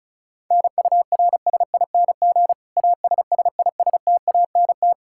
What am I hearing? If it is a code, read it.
Morse code, heard as a keyed tone, about 36 wpm, 700 Hz. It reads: NURSING ASSISTANT
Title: Practice QSO - Exchange 3 - More Info - 35wpm